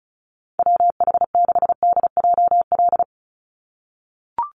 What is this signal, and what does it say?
Morse code, heard as a keyed tone, about 34 wpm, 700 Hz. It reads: WH6DJL
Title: USA Call Signs 35wpm